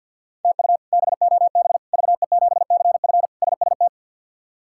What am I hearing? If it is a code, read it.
Morse code, heard as a keyed tone, about 50 wpm, 700 Hz. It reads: TU BOB VE7CF SST
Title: K1USN SST QSO Practice - 50wpm